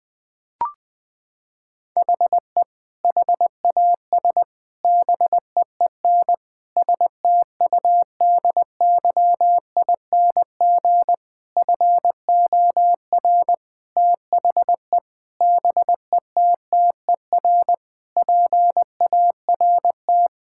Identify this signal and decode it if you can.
Morse code, heard as a keyed tone, about 21 wpm, 700 Hz. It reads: HE HAS BEEN STUDYING FOR THE BETTER PART
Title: Sentences with 7-Word Phrases - 20wpm